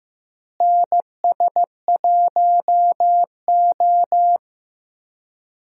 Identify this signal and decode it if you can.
Morse code, heard as a keyed tone, about 15 wpm, 700 Hz. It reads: NS1O
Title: USA Call Signs - 4 Characters - Repeated 3x - 15wpm